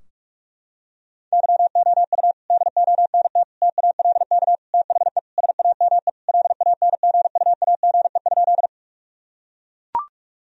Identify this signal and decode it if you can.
Morse code, heard as a keyed tone, about 42 wpm, 700 Hz. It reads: YOU DONT TALK THE SAME LANGUAGE?
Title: Sentences from Top 1000 Words - Review Part 3 - 45wpm